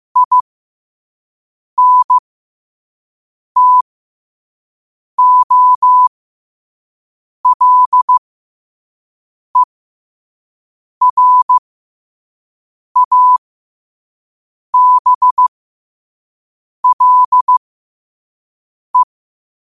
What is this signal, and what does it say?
Morse code, heard as a keyed tone, about 15 wpm, spaced as for 5 wpm, 1000 Hz. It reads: INTOLERABLE
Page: New quotes every day in morse code at 5 Words per minute.